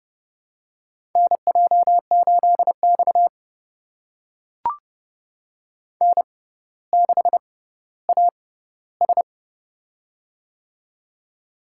Morse code, heard as a keyed tone, about 30 wpm, spaced as for 15 wpm, 700 Hz. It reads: NJ8X N6AS
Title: USA Call Signs - 4 Characters - 6x Character Spacing - 30wpm